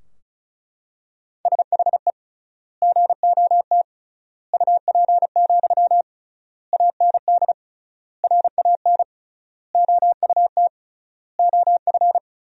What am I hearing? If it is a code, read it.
Morse code, heard as a keyed tone, about 36 wpm, 700 Hz. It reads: SHE GOT UP, AND RAN OUT OF
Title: 3x Word-Spacing - Sentences from Top 500 Words - 35wpm